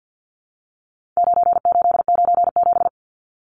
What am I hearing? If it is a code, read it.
Morse code, heard as a keyed tone, about 50 wpm, 700 Hz. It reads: 9897
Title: Numbers - 4-Digit 50wpm